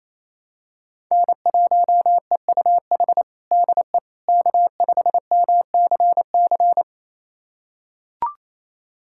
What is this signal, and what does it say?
Morse code, heard as a keyed tone, about 28 wpm, 700 Hz. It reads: N1EUH DE K5MCC
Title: USA Callsign DE Callsign - 5 Characters - 28wpm